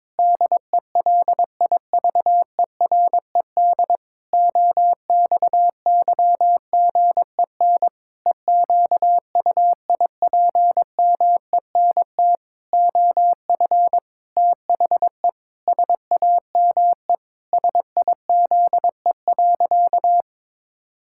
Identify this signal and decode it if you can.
Morse code, heard as a keyed tone, about 22 wpm, 700 Hz. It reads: DELIVERED OXYGEN EQUIPMENT OF THE SAME SIZE.
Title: Pangram Challenge v2 - 22wpm